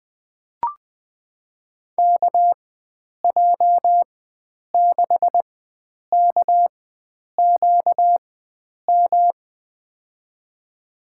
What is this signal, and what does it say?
Morse code, heard as a keyed tone, about 20 wpm, spaced as for 9 wpm, 700 Hz. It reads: KJ6KQM
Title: USA Call Signs - 6 Characters - 4x Character Spacing - 20wpm